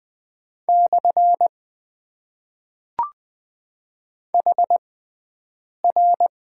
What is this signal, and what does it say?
Morse code, heard as a keyed tone, about 20 wpm, spaced as for 7 wpm, 700 Hz. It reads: / HR
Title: Top 90 QSO Elements - 6x Character Spacing - 20wpm